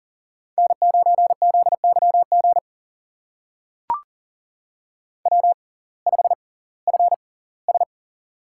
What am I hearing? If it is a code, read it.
Morse code, heard as a keyed tone, about 40 wpm, spaced as for 22 wpm, 700 Hz. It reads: N9ZYG W5FS
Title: USA Call Signs - 6x Character Spacing - 40wpm